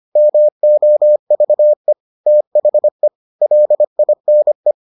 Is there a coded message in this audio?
Yes: Morse code from 0.2 to 4.7 s, heard as a keyed tone, about 25 wpm, 600 Hz. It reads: MOVE THE LINE